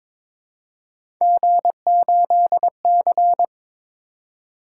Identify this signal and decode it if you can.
Morse code, heard as a keyed tone, about 22 wpm, 700 Hz. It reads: G8C
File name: practice / 22wpm / Three Letter-Number Combinations - 22wpm